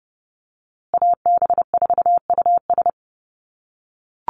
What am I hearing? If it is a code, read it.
Morse code, heard as a keyed tone, about 30 wpm, 700 Hz. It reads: AB4US
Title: USA Call Signs Repeated 2x 30wpm